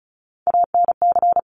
Morse code, heard as a keyed tone, about 36 wpm, 700 Hz. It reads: ANC
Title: Top Three Letter Combinations in Words 35wpm